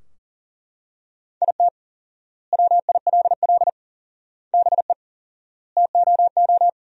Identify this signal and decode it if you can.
Morse code, heard as a keyed tone, about 40 wpm, 700 Hz. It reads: IT WILL BE TOO
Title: 4x Word-Spacing - Sentences from Top 300 Words - 40wpm